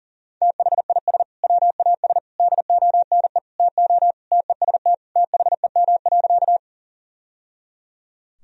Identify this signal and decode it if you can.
Morse code, heard as a keyed tone, about 40 wpm, 700 Hz. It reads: THIS WAS DONE TO TEST THEM.
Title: Sentences from Top 1000 Words - Review Part 3 - 40wpm